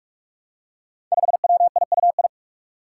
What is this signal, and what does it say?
Morse code, heard as a keyed tone, about 46 wpm, 700 Hz. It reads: 5WIUI
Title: Five Challenging Letter-Number Combinations 45wpm